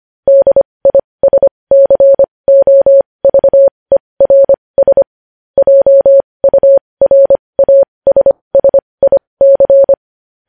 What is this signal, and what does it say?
Morse code, heard as a keyed tone, about 25 wpm, spaced as for 20 wpm, 550 Hz. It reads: DISCOVERS JURASSIC
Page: news headlines in morse code at 20 wpm